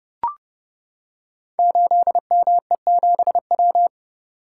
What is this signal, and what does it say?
Morse code, heard as a keyed tone, about 30 wpm, 700 Hz. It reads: 8ME7W
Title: Code Groups - 5 Letter:Numbers - 30wpm